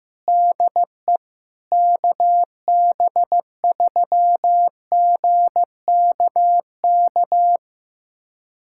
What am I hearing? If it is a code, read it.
Morse code, heard as a keyed tone, about 15 wpm, 700 Hz. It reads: DE KB3GKK
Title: USA Callsign DE Callsign 15wpm